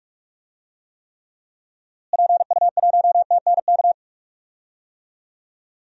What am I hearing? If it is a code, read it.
Morse code, heard as a keyed tone, about 45 wpm, 700 Hz. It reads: PU1TNK
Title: DXCC-100 Call Signs - 6 Characters - Repeated 3x - 45wpm